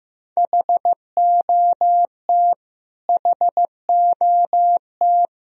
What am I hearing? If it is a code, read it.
Morse code, heard as a keyed tone, about 15 wpm, 700 Hz. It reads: HOT HOT
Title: Top 100 Words Repeated 15wpm